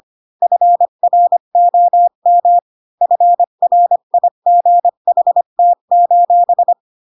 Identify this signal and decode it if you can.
Morse code, heard as a keyed tone, about 25 wpm, 700 Hz. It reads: FROM FRIGHT:
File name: Pinocchio / Pinocchio ch1 20 25 ik1hge